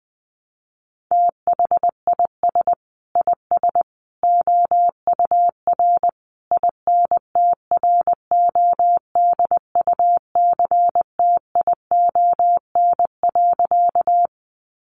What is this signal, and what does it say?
Morse code, heard as a keyed tone, about 20 wpm, 700 Hz. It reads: THIS IS OUR INTRODUCTION.
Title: Sentences from Top 1000 Words 20wpm